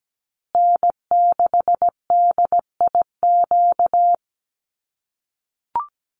Morse code, heard as a keyed tone, about 17 wpm, 700 Hz. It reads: N6DIQ
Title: USA Call Signs Repeated 3x 17wpm